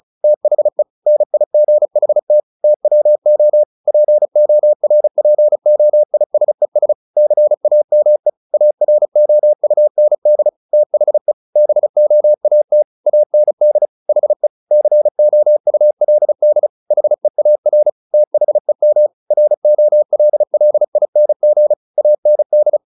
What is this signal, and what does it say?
Morse code, heard as a keyed tone, about 34 wpm, 600 Hz. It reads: THE NIGHT TWO PORPOISES CAME AROUND THE BOAT AND HE COULD HEAR THEM ROLLING AND
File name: the old man and the sea / the old man and the sea 1 35wpm